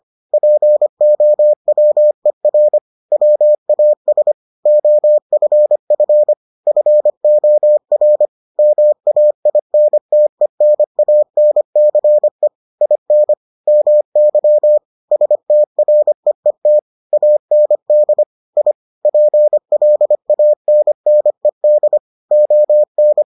Morse code, heard as a keyed tone, about 25 wpm, 600 Hz. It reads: POWER WAS OFF FOR MAINTENANCE IN MY STREET AND I PLANNED ON